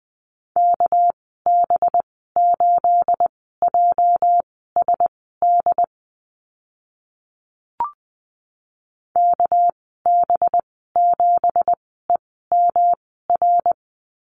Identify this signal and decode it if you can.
Morse code, heard as a keyed tone, about 20 wpm, spaced as for 14 wpm, 700 Hz. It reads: KB8JSD KB7EMR
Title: 245 Lesson 39g - Letter z - Call Signs 20wpm